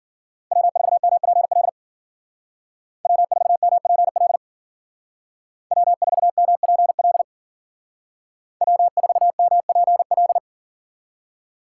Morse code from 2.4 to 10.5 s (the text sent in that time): W4MPL W4MPL W4MPL